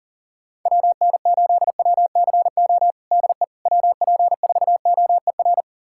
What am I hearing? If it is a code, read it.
Morse code, heard as a keyed tone, about 40 wpm, 700 Hz. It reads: WN8WCO DE WP4OER
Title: USA Callsign DE Callsign - 6 Characters - 40wpm